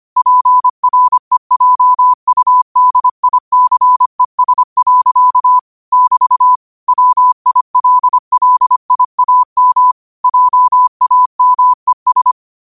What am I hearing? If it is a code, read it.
Morse code, heard as a keyed tone, about 25 wpm, 1000 Hz. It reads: PREJUDICES. <BT> WILLIAM JAMES